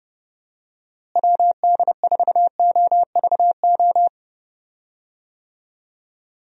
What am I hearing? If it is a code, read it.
Morse code, heard as a keyed tone, about 31 wpm, 700 Hz. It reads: WD4OVO
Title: USA Call Signs - 6 Characters - Repeated 2x - 30wpm